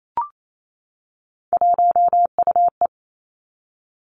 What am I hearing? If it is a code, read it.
Morse code, heard as a keyed tone, about 28 wpm, 700 Hz. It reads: 1UE
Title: Three Challenging Letter-Number Combinations 28wpm